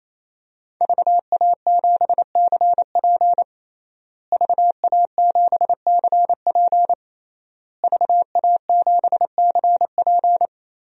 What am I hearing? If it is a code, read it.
Morse code, heard as a keyed tone, about 27 wpm, 700 Hz. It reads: VA7CP VA7CP VA7CP
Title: DXCC-100 Call Signs - 5 Characters - Repeated 3x - 28wpm